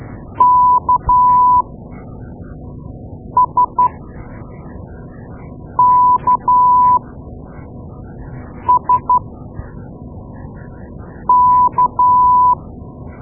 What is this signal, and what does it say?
Morse code, heard as a keyed tone, about 12 wpm, spaced as for 4 wpm, 990 Hz. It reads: KSKSK
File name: Morse Lessons / Morse Lessons / Tape-2A